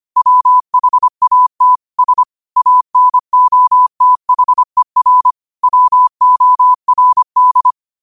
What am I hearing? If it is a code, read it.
Morse code, heard as a keyed tone, about 25 wpm, 1000 Hz. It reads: WHAT S ANOTHER WORD
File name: 25wpm